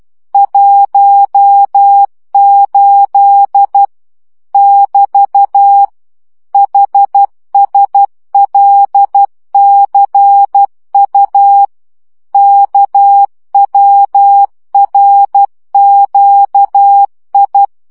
Morse code, heard as a keyed tone, about 12 wpm, 800 Hz. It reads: 18 <BT> HSLCU KWRQI
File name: morze vizsga 2011-3 10